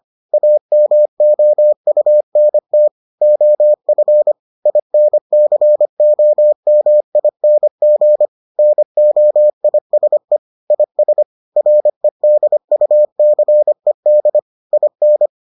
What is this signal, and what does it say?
Morse code, heard as a keyed tone, about 25 wpm, 600 Hz. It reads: AMOUNT OF INCOMING NOISE IS REDUCED IN